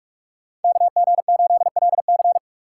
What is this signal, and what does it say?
Morse code, heard as a keyed tone, about 58 wpm, 690 Hz. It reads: KG8LC